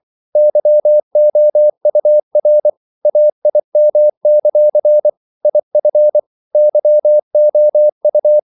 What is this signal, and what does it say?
Morse code, heard as a keyed tone, about 24 wpm, 600 Hz. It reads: YOUR AIM; IF YOU
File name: CW / Practice / If24